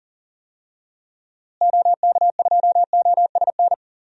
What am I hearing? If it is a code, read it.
Morse code, heard as a keyed tone, about 40 wpm, 700 Hz. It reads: OK2OSN
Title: DXCC-100 Call Signs - 6 Characters - Repeated 3x - 40wpm